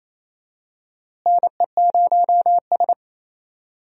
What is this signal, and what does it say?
Morse code, heard as a keyed tone, about 27 wpm, 700 Hz. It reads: NE0S
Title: USA Call Signs - 4 Characters - Repeated 3x - 28wpm